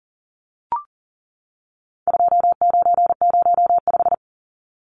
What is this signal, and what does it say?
Morse code, heard as a keyed tone, about 39 wpm, 700 Hz. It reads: 2905
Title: Numbers - 4-Digit 40wpm